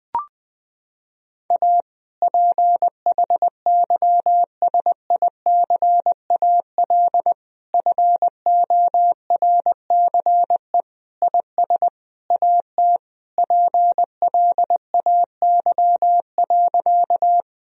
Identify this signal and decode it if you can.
Morse code, heard as a keyed tone, about 21 wpm, 700 Hz. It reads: A PHYSICAL FORCE IS AT PLAY.
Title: Sentences from Top 1000 Words - Review Part 4 - 20wpm